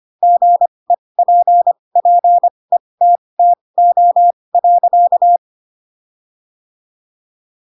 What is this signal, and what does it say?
Morse code, heard as a keyed tone, about 25 wpm, spaced as for 20 wpm, 700 Hz. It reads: GEPPETTO.
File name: Pinocchio / Pinocchio ch6 15 25 ik1hge